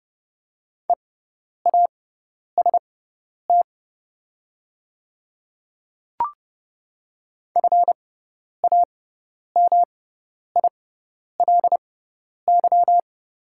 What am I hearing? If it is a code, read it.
Morse code, heard as a keyed tone, about 30 wpm, spaced as for 10 wpm, 690 Hz. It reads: EAST FAMILY
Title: 6x Character Spacing - Top 500 Words - 30wpm